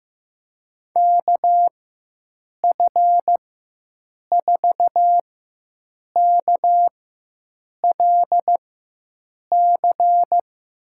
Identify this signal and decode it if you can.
Morse code, heard as a keyed tone, about 16 wpm, spaced as for 7 wpm, 700 Hz. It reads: KF4KLC
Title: USA Call Signs - 4x Character Spacing - 15wpm